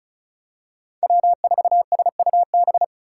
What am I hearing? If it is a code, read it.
Morse code, heard as a keyed tone, about 34 wpm, 700 Hz. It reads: W4SUB
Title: USA Call Signs - 5 Characters - 6x Character Spacing - 35wpm